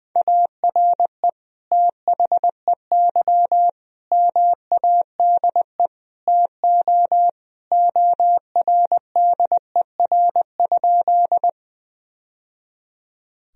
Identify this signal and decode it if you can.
Morse code, heard as a keyed tone, about 20 wpm, 700 Hz. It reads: ARE THEY MADE TO ORDER?
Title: Sentences from Top 1000 Words - Review Part 2 - 20wpm